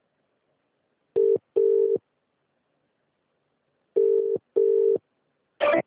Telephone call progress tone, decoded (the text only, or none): ringback tone